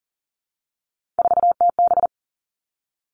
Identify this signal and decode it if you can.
Morse code, heard as a keyed tone, about 40 wpm, 700 Hz. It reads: 4TB
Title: All Cut Numbers - 3-Digit 40wpm